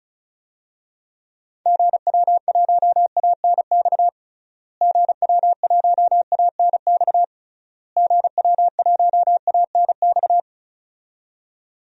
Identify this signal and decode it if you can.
Morse code, heard as a keyed tone, about 36 wpm, 700 Hz. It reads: GW1ANX GW1ANX GW1ANX
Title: DXCC-100 Call Signs - 6 Characters - Repeated 3x - 35wpm